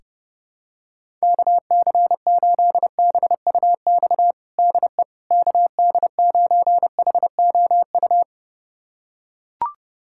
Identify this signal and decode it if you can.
Morse code, heard as a keyed tone, about 30 wpm, 700 Hz. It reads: KC8BUX DE KD9HOU
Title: USA Callsign DE Callsign - 6 Characters - 30wpm